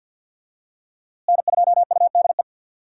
Morse code, heard as a keyed tone, about 52 wpm, 700 Hz. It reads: N2UDE
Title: USA Call Signs - 5 Characters - Repeated 3x - 50wpm